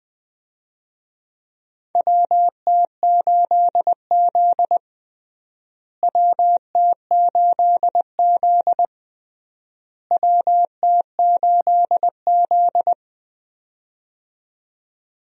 Morse code, heard as a keyed tone, about 20 wpm, 700 Hz. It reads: WT8Z WT8Z WT8Z
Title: USA Call Signs - 4 Characters - Repeated 3x - 20wpm